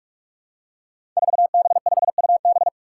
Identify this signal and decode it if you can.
Morse code, heard as a keyed tone, about 42 wpm, 700 Hz. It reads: 4B5UB
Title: Five Challenging Letter-Number Combinations 45wpm